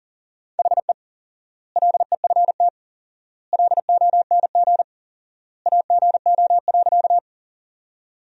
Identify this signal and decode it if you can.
Morse code, heard as a keyed tone, about 40 wpm, 700 Hz. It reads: HE LEFT LONG AGO.
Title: 4x Word-Spacing - Sentences from Top 500 Words - 40wpm